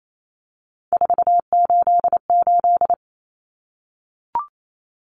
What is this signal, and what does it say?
Morse code, heard as a keyed tone, about 28 wpm, 700 Hz. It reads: 488
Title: Numbers - 3-Digit 28wpm